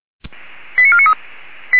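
Five-tone selective call (ZVEI), heard as quick runs of five tones, tones of about 70 ms. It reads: 89493
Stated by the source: ZVEI five-tone selective call received with an SDR.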